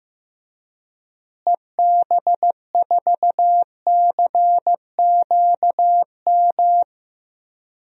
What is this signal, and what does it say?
Morse code, heard as a keyed tone, about 15 wpm, 700 Hz. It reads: EB4CQM
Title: DXCC-100 Call Signs - 6 Characters - Repeated 3x - 15wpm